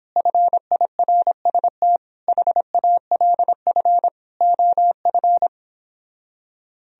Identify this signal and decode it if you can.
Morse code, heard as a keyed tone, about 29 wpm, 700 Hz. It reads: FIRST HALF OF